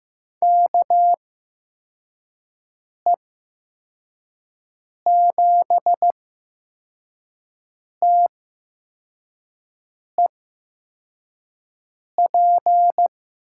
Morse code, heard as a keyed tone, about 15 wpm, spaced as for 4 wpm, 700 Hz. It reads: KE7TEP
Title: USA Call Signs - 6 Characters - 8x Character Spacing - 15wpm